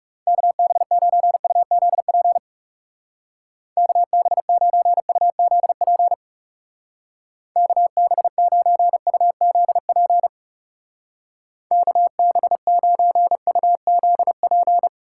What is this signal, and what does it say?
Morse code, heard as a keyed tone, about 35 wpm, 700 Hz. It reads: KB9UZP KB9UZP KB9UZP KB9UZP
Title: Speed Racing - USA 2x3 Call Signs 30wpm